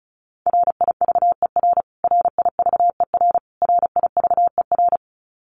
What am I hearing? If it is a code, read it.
Morse code, heard as a keyed tone, about 34 wpm, 700 Hz. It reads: RIVER RIVER RIVER
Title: Top 300 Words Repeated 35wpm